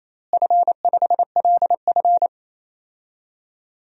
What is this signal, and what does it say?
Morse code, heard as a keyed tone, about 27 wpm, 700 Hz. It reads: F5LF
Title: DXCC-100 Call Signs - 4 Characters - Repeated 3x - 28wpm